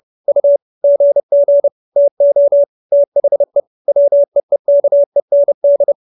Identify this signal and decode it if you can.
Morse code, heard as a keyed tone, about 30 wpm, 570 Hz. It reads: U GG TO THE WEEKEND